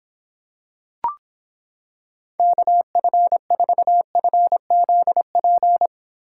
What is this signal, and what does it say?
Morse code, heard as a keyed tone, about 26 wpm, 700 Hz. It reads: KF4FZP